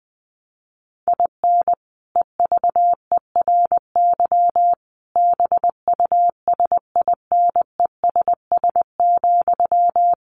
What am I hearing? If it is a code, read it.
Morse code, heard as a keyed tone, about 20 wpm, 700 Hz. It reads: IN EVERY BUSINESS,
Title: Sentences from Top 1400 Words 20wpm